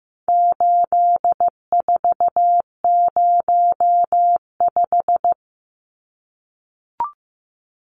Morse code, heard as a keyed tone, about 15 wpm, 700 Hz. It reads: 8405
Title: Numbers - 4-Digit 15wpm